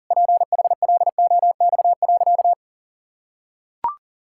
Morse code, heard as a keyed tone, about 40 wpm, 700 Hz. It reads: PHLOX.
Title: Pangram Challenge v2 - 40wpm